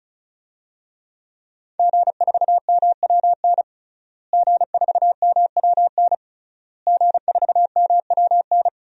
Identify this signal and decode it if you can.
Morse code, heard as a keyed tone, about 36 wpm, 700 Hz. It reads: G4MWN G4MWN G4MWN
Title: DXCC-100 Call Signs - 5 Characters - Repeated 3x - 35wpm